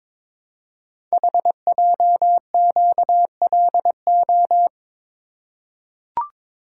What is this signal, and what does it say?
Morse code, heard as a keyed tone, about 22 wpm, 700 Hz. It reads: HJQLO
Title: Code Groups - 5 Letters - 22wpm